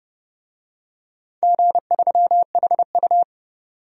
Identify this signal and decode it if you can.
Morse code, heard as a keyed tone, about 30 wpm, 700 Hz. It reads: G3HU
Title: DXCC-100 Call Signs - 4 Characters - 30wpm